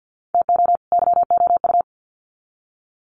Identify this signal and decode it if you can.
Morse code, heard as a keyed tone, about 50 wpm, 700 Hz. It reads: TO YOU
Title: Sets of 2 Words 50wpm